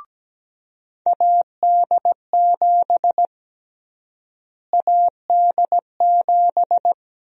Morse code, heard as a keyed tone, about 17 wpm, 700 Hz. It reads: AD7 AD7
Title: USA Call Sign Prefixes - Repeated 2x - 17wpm